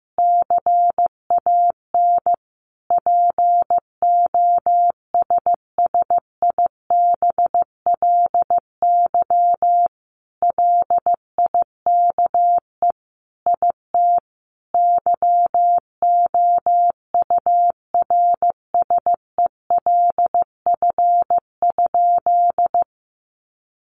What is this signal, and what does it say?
Morse code, heard as a keyed tone, about 15 wpm, 700 Hz. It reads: CAN POSSIBLY LIKE IT YOURSELF?
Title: Sentences from Top 1300 Words 15wpm